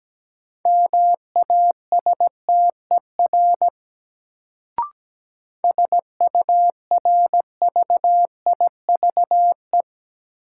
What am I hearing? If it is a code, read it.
Morse code, heard as a keyed tone, about 17 wpm, 700 Hz. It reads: MASTER SURVIVE
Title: Sentences from Top 1300 Words - New Word Drill - 17wpm